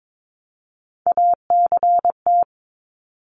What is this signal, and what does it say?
Morse code, heard as a keyed tone, about 22 wpm, 700 Hz. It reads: ACT
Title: Top 300 Words 22wpm